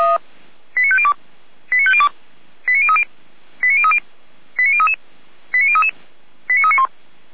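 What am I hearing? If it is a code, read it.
Five-tone selective call (ZVEI), heard as quick runs of five tones, tones of about 80 ms. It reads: 89502 89502 89030 89030 89033 89033 89391
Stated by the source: ZVEI five-tone selective call received with an SDR.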